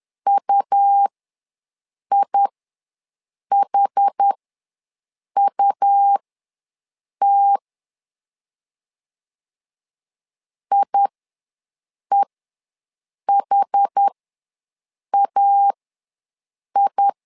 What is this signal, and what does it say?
Morse code, heard as a keyed tone, about 10 wpm, spaced as for 6 wpm, 800 Hz. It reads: UIHUT IEHAI
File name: LEZIONI / 2 UHDN / 2 UH ws 10wpm 2029